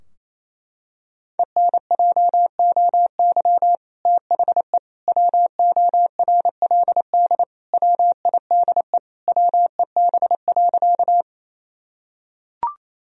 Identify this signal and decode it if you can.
Morse code, heard as a keyed tone, about 28 wpm, 700 Hz. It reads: ENJOY THE WORLD WIDE WEB.
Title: Sentences from Top 1000 Words - Review Part 4 - 28wpm